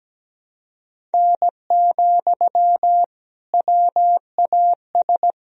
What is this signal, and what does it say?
Morse code, heard as a keyed tone, about 17 wpm, 700 Hz. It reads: N, WAS
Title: Sentences from Top 100 Words and Abbreviations 17wpm